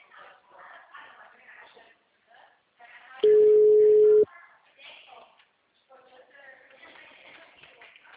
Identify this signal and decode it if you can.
Ringback tone (Europe, 425 Hz), with speech.